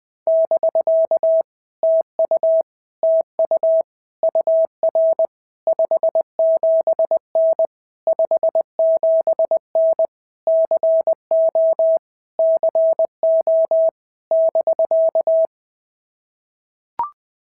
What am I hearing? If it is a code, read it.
Morse code, heard as a keyed tone, about 20 wpm, 650 Hz. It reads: <BK> TU TU UR 57N 57N CO CO <BK>
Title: POTA QSO Practice - 20wpm